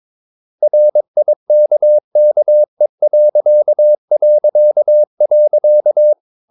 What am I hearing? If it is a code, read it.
Morse code, heard as a keyed tone, about 22 wpm, 600 Hz. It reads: RIKKE...
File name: Ferie QRS 22wpm